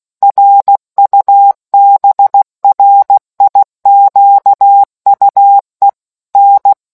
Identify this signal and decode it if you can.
Morse code, heard as a keyed tone, about 16 wpm, 800 Hz. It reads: RUBRIQUE N